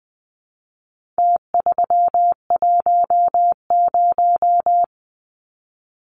Morse code, heard as a keyed tone, about 20 wpm, 700 Hz. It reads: T310
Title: Numbers - Contest Style - 4-Digit 20wpm